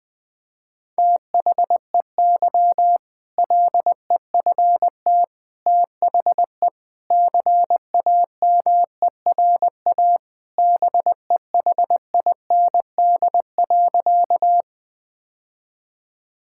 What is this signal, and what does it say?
Morse code, heard as a keyed tone, about 20 wpm, 700 Hz. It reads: THEY LEFT THE CAMERA BEHIND.
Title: Sentences from Top 1000 Words - Review Part 5 - 20wpm